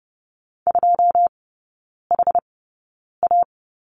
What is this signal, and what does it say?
Morse code, heard as a keyed tone, about 30 wpm, spaced as for 9 wpm, 700 Hz. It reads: 2HA
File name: 081 Lesson 14e - Number 2 - Three Characters 30wpm